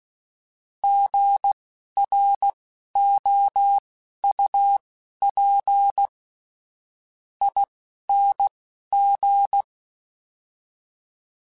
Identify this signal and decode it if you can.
Morse code, heard as a keyed tone, about 16 wpm, spaced as for 10 wpm, 800 Hz. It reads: GROUPING
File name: Alphabetical / Alphabetical mp3s / Gs @10